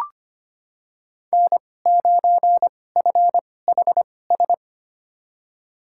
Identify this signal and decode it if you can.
Morse code, heard as a keyed tone, about 25 wpm, spaced as for 18 wpm, 700 Hz. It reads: N9FHS
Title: USA Call Signs - 5 Characters - 2x Character Spacing - 25wpm